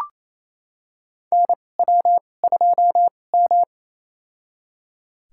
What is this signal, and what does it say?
Morse code, heard as a keyed tone, about 28 wpm, spaced as for 20 wpm, 700 Hz. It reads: NW2M
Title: USA Call Signs - 4 Characters - 2x Character Spacing - 28wpm